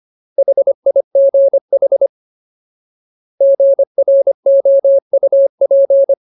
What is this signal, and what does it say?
Morse code, heard as a keyed tone, about 25 wpm, 550 Hz. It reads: HIGH GROUP
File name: Ultimate Word List / 25WPM Ultimate